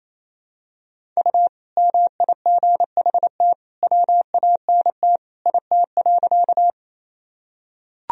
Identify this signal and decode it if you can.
Morse code, heard as a keyed tone, about 28 wpm, 700 Hz. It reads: U MIGHT WANT IT.
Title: Sentences from Top 300 Words and Abbreviations 28wpm